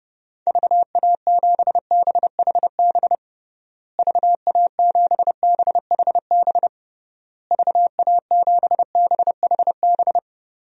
Morse code, heard as a keyed tone, about 30 wpm, 700 Hz. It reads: VA7BHB VA7BHB VA7BHB
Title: DXCC-100 Call Signs - 6 Characters - Repeated 3x - 30wpm